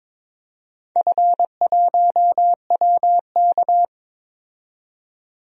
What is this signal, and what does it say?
Morse code, heard as a keyed tone, about 22 wpm, 700 Hz. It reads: F1WK
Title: DXCC-100 Call Signs - 4 Characters - 22wpm